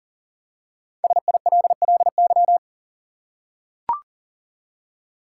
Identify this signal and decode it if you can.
Morse code, heard as a keyed tone, about 40 wpm, 700 Hz. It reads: SILLY
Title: Five Letter Words - 40wpm